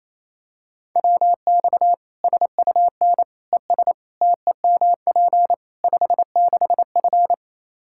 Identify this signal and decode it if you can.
Morse code, heard as a keyed tone, about 28 wpm, 700 Hz. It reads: WX SUN ES TEMP 56F
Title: Practice QSO - Exchange 2 - Info - 28wpm